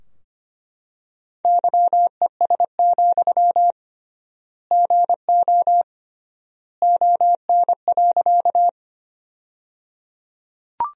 Morse code, heard as a keyed tone, about 25 wpm, 700 Hz. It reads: YES, GO ON.
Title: 3x Word-Spacing - Sentences from Top 500 Words - 25wpm